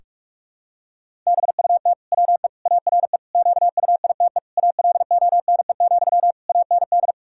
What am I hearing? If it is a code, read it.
Morse code, heard as a keyed tone, about 46 wpm, 700 Hz. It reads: BUT WE ARE QUITE ALONE, AND
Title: Sentences from Top 1600 Words 45wpm